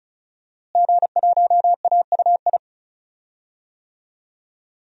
Morse code, heard as a keyed tone, about 36 wpm, 700 Hz. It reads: G1AUI
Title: DXCC-100 Call Signs - 5 Characters - Repeated 2x - 35wpm